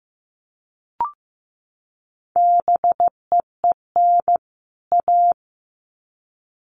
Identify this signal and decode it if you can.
Morse code, heard as a keyed tone, about 15 wpm, 700 Hz. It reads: BEEN A
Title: Sets of 2 Words 15wpm